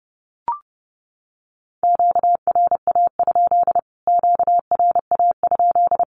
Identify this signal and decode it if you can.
Morse code, heard as a keyed tone, about 30 wpm, 700 Hz. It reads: QRA? QRA?
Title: Q-Codes Repeated 3x 30wpm